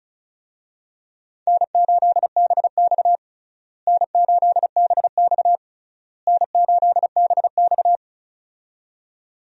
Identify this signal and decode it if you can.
Morse code, heard as a keyed tone, about 36 wpm, 700 Hz. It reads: N8BX N8BX N8BX
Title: USA Call Signs - 4 Characters - Repeated 3x - 35wpm